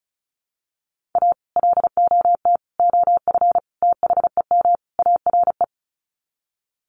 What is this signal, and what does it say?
Morse code, heard as a keyed tone, about 34 wpm, 700 Hz. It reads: A LOT OF THEM ARE
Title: Sets of 5 Words Encore2 35wpm